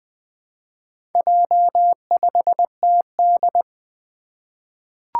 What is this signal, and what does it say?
Morse code, heard as a keyed tone, about 20 wpm, 700 Hz. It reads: J5TD
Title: Four Challenging Letter-Number Combinations 20wpm